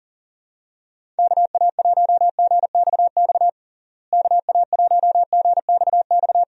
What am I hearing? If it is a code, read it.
Morse code, heard as a keyed tone, about 40 wpm, 700 Hz. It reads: KA1GXX KA1GXX
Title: USA Call Signs - 6 Characters - Repeated 2x - 40wpm